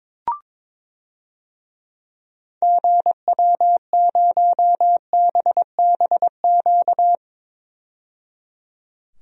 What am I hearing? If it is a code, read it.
Morse code, heard as a keyed tone, about 22 wpm, 700 Hz. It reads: GW0BBQ
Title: DXCC-100 Call Signs - 6 Characters - 22wpm